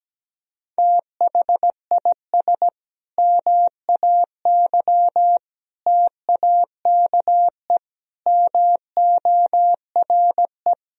Morse code, heard as a keyed tone, about 17 wpm, 700 Hz. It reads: THIS MAY TAKE MORE
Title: Sentences from Top 1000 Words - Review Part 3 - 17wpm